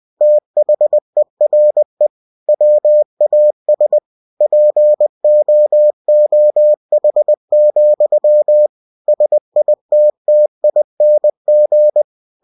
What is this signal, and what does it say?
Morse code, heard as a keyed tone, about 20 wpm, 600 Hz. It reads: THERE WAS POOH, SITTING
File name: cw-books / winnie-the-pooh / winnie-20wpm-page-0094